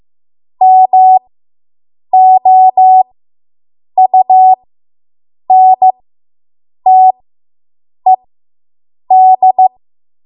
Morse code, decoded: MOUNTED